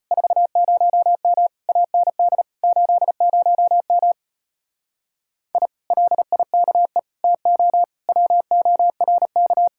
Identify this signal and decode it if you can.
Morse code, heard as a keyed tone, about 33 wpm, 700 Hz. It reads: 40M AND 80M I LIKE TO WORK